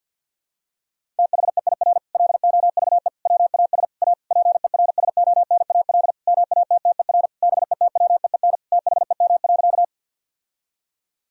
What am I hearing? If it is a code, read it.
Morse code, heard as a keyed tone, about 50 wpm, 700 Hz. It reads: THEIR LOVE WAS A PERSONAL MATTER BETWEEN THEM.
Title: Sentences from Top 1000 Words - Review Part 4 - 50wpm